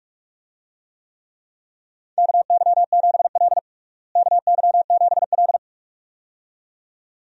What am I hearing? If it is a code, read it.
Morse code, heard as a keyed tone, about 41 wpm, 700 Hz. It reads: KY7L KY7L
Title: USA Call Signs - 4 Characters - Repeated 2x - 45wpm